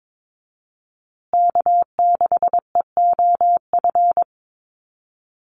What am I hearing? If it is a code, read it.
Morse code, heard as a keyed tone, about 22 wpm, 700 Hz. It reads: K6EOF
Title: USA Call Signs Repeated 3x 22wpm